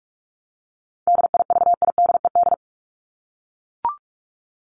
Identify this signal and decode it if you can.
Morse code, heard as a keyed tone, about 44 wpm, 700 Hz. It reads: DIVIDED
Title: ED Suffix - Top 100 Words with Base Words 45wpm